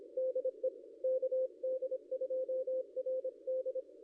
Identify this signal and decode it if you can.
Morse code, heard as a keyed tone, about 26 wpm, 520 Hz. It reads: DE KD2RD
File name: contests / ss-cw-k5tr-op-n5rz -0500